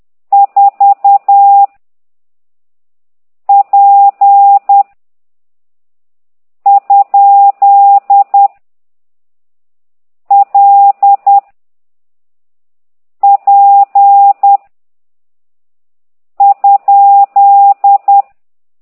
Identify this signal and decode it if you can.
Morse code, heard as a keyed tone, about 10 wpm, 800 Hz. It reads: 4 P ? L P ?
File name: ct2idw / exerc15